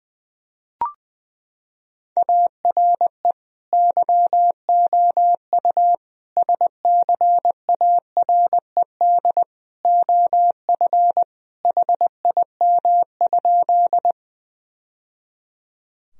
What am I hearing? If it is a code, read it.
Morse code, heard as a keyed tone, about 21 wpm, 700 Hz. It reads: ARE YOU SCARED OF HIM?
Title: Sentences from Top 1000 Words - Review Part 4 - 20wpm